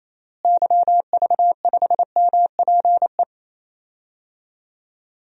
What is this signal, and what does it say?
Morse code, heard as a keyed tone, about 28 wpm, 700 Hz. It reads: YV5MPE
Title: DXCC-100 Call Signs - 6 Characters - Repeated 3x - 28wpm